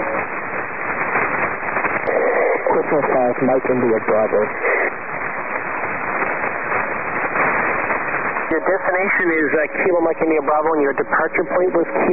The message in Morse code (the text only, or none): none